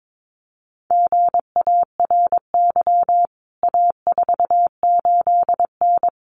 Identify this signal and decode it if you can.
Morse code, heard as a keyed tone, about 22 wpm, 700 Hz. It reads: GARY A48N
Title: CWT Exchange 22wpm